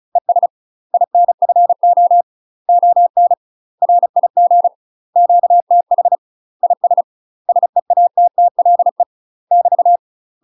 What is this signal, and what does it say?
Morse code, heard as a keyed tone, about 35 wpm, 700 Hz. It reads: ES INFO ON RIG QTH IS SEATTLE <BT>
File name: UK MASTER QSOs TRAINING / 35WPM / UK MASTER QSOs 0012